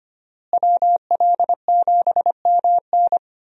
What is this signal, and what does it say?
Morse code, heard as a keyed tone, about 25 wpm, 700 Hz. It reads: WL7MN
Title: USA Call Signs - 5 Characters - Repeated 3x - 25wpm